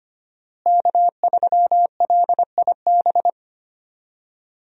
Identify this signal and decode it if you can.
Morse code, heard as a keyed tone, about 25 wpm, 700 Hz. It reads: K3LIB
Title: USA Call Signs - 5 Characters - Repeated 3x - 25wpm